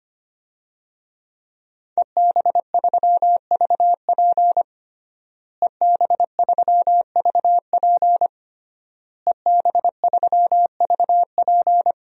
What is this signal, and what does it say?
Morse code, heard as a keyed tone, about 25 wpm, 700 Hz. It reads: EB3VP EB3VP EB3VP
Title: DXCC-100 Call Signs - 5 Characters - Repeated 3x - 25wpm